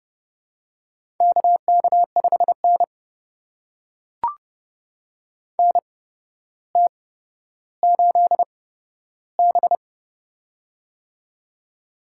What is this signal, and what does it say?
Morse code, heard as a keyed tone, about 28 wpm, spaced as for 13 wpm, 700 Hz. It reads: KK5N NT8B
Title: USA Call Signs - 4 Characters - 8x Character Spacing - 30wpm